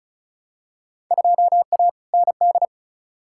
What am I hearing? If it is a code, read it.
Morse code, heard as a keyed tone, about 34 wpm, 700 Hz. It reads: 2A ND
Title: Field Day Exchanges Encore - 35wpm